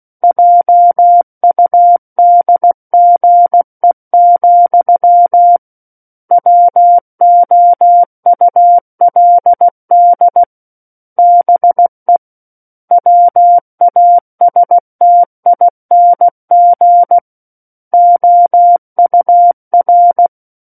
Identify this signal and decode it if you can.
Morse code, heard as a keyed tone, about 16 wpm, 700 Hz. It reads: JUDGE, WOULD BE WASTING OUR
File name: 03-Chapter-03 16 WPM